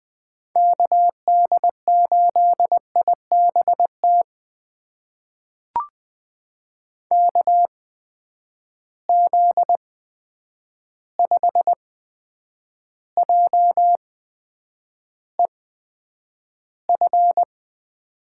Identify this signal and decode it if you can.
Morse code, heard as a keyed tone, about 20 wpm, spaced as for 9 wpm, 700 Hz. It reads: KD8IBT KZ5JEF
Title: USA Call Signs - 6 Characters - 8x Character Spacing - 20wpm